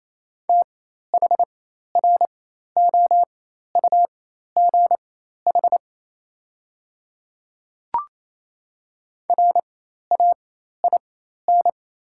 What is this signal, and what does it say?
Morse code, heard as a keyed tone, about 28 wpm, spaced as for 13 wpm, 700 Hz. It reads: THROUGH RAIN
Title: 4x Character Spacing - Top 300 Words - 28wpm